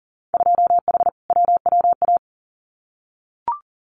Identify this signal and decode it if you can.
Morse code, heard as a keyed tone, about 41 wpm, 700 Hz. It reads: 2H WWA
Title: Winter Field Day Exchanges 40wpm